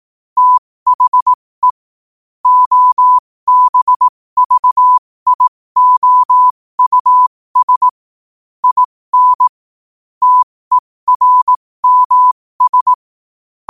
Morse code, heard as a keyed tone, about 18 wpm, spaced as for 15 wpm, 1000 Hz. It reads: THE OBVIOUS IN TERMS
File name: qotdfiles / 15wpm